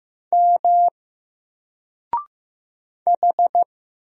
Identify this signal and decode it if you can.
Morse code, heard as a keyed tone, about 15 wpm, 700 Hz. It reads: M H
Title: Single Character (Letter:Number:Punctuation:Prosign) 15wpm